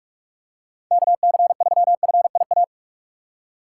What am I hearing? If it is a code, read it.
Morse code, heard as a keyed tone, about 46 wpm, 700 Hz. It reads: KC3FIA
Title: USA Call Signs - 6 Characters - 3x Character Spacing - 45wpm